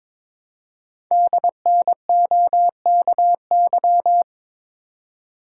Morse code, heard as a keyed tone, about 21 wpm, 700 Hz. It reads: DNOKY
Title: Code Groups - 5 Letters - 22wpm